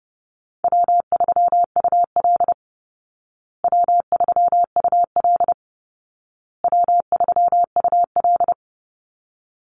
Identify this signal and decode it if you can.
Morse code, heard as a keyed tone, about 30 wpm, 700 Hz. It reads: W3UL W3UL W3UL
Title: Easy Call Signs Repeated 3x - 30wpm